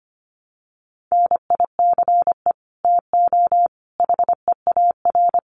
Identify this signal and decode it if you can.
Morse code, heard as a keyed tone, about 25 wpm, 700 Hz. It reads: NICE TO HEAR
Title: Common QSO Phrases 25wpm